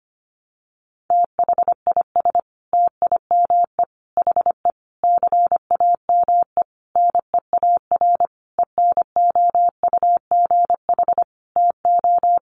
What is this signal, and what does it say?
Morse code, heard as a keyed tone, about 25 wpm, 700 Hz. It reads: THIS TIME HE CAME NEAR ENOUGH TO
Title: Sentences from Top 1200 Words 25wpm